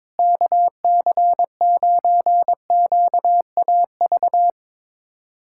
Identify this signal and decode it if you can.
Morse code, heard as a keyed tone, about 21 wpm, 700 Hz. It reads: KC9QAV
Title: USA Callsign DE Callsign - 6 Characters - 22wpm